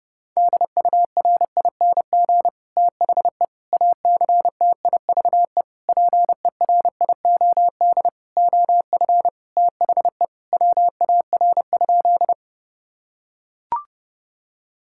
Morse code, heard as a keyed tone, about 30 wpm, 700 Hz. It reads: DURING THE ACTIVE PERIOD OF THE WAR?
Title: Sentences from Top 1000 Words - Review Part 6 - 30wpm